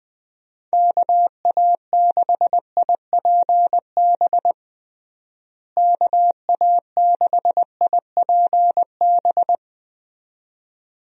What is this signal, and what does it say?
Morse code, heard as a keyed tone, about 21 wpm, 700 Hz. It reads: KA6IPB KA6IPB
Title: USA Call Signs - 6 Characters - Repeated 2x - 20wpm